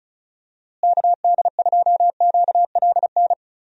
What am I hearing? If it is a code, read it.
Morse code, heard as a keyed tone, about 34 wpm, 700 Hz. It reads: KD2QLN
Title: USA Call Signs - 6 Characters - Repeated 2x - 35wpm